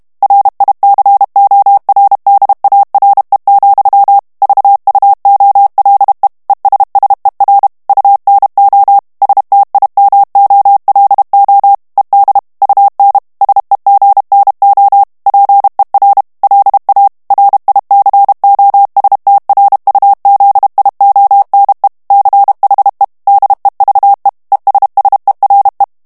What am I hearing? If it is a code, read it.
Morse code, heard as a keyed tone, about 32 wpm, 800 Hz. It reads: RICORDARE, VUOLE ESSER UNO STIMOLO ED UN SEGNO PER LA RICOSTRUZIONE CHE DEVE ESSERE
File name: ITALIEN texte clair 1 vitesse 30 mots minute